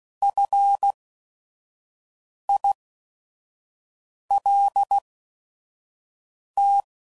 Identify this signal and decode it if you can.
Morse code, heard as a keyed tone, about 16 wpm, spaced as for 5 wpm, 800 Hz. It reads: FILT
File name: CW Code Groups @5WPM